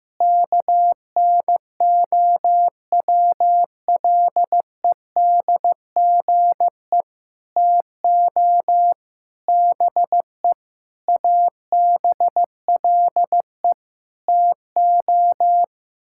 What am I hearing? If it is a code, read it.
Morse code, heard as a keyed tone, about 15 wpm, 700 Hz. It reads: KNOWLEDGE TO BE ABLE TO
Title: Sentences from Top 1000 Words - Review Part 6 - 15wpm